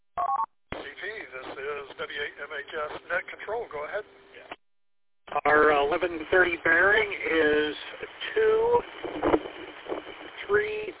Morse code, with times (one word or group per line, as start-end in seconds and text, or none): none